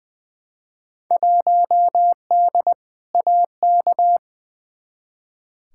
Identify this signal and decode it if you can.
Morse code, heard as a keyed tone, about 21 wpm, 700 Hz. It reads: 1D AK
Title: Field Day Exchanges - 20wpm